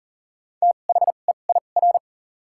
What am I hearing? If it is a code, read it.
Morse code, heard as a keyed tone, about 40 wpm, spaced as for 29 wpm, 700 Hz. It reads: THEIR
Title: 2x Character Spacing - Top 100 Words - 40wpm